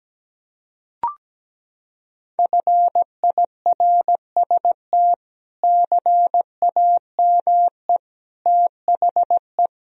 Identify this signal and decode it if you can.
Morse code, heard as a keyed tone, about 17 wpm, 700 Hz. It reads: FIRST CAME THE
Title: Sentences from Top 1000 Words - Review Part 1 - 17wpm